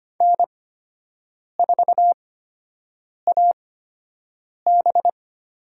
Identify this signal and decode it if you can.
Morse code, heard as a keyed tone, about 25 wpm, spaced as for 7 wpm, 700 Hz. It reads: N4AB
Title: USA Call Signs - 8x Character Spacing - 25wpm